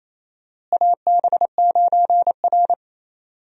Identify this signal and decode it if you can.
Morse code, heard as a keyed tone, about 28 wpm, 700 Hz. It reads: AB9R
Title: USA Call Signs - 4 Characters - Repeated 3x - 28wpm